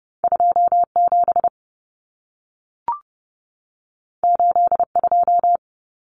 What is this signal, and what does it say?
Morse code, heard as a keyed tone, about 30 wpm, 700 Hz. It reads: 27 82
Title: Numbers - 2-Digit 30wpm